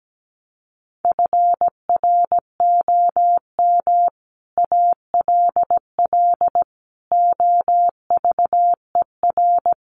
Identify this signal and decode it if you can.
Morse code, heard as a keyed tone, about 17 wpm, 700 Hz. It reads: FROM ALL OVER
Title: Sets of 5 Words Encore 17wpm